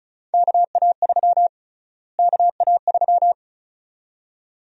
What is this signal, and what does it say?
Morse code, heard as a keyed tone, about 36 wpm, 700 Hz. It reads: KA3 KA3
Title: USA Call Sign Prefixes - Repeated 2x - 35wpm